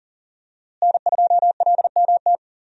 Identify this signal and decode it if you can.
Morse code, heard as a keyed tone, about 40 wpm, 690 Hz. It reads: N2LMT
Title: USA Call Signs - 5 Characters - Repeated 3x - 40wpm